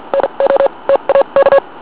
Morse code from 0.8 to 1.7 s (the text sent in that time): TAK